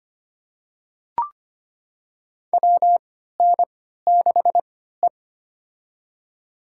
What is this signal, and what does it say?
Morse code, heard as a keyed tone, about 25 wpm, spaced as for 14 wpm, 700 Hz. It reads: WN6E
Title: USA Call Signs - 4 Characters - 3x Character Spacing - 25wpm